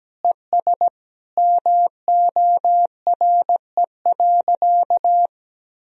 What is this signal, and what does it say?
Morse code, heard as a keyed tone, about 17 wpm, 700 Hz. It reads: ES MORE.
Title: Sentences from Top 500 Words and Abbreviations 17wpm